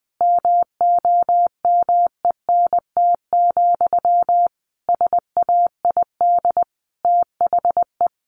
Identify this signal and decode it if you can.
Morse code, heard as a keyed tone, about 20 wpm, 700 Hz. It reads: MOMENT, SAID THE
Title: Sentences from Top 1000 Words 20wpm